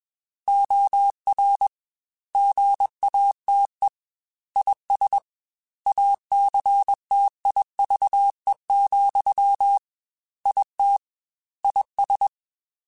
Morse code, heard as a keyed tone, about 21 wpm, 800 Hz. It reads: OR GATE IS ACTIVE, IT IS
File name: Technical Subjects / Technical Subjects-mp3s / AND Gate @20